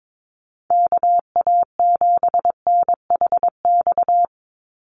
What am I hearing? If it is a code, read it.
Morse code, heard as a keyed tone, about 22 wpm, 700 Hz. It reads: KA7NHX
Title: USA Call Signs 22wpm